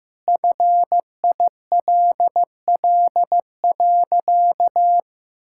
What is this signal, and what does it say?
Morse code, heard as a keyed tone, about 16 wpm, 700 Hz. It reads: FILL.
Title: Sentences from Top 1000 Words - Review Part 5 - 15wpm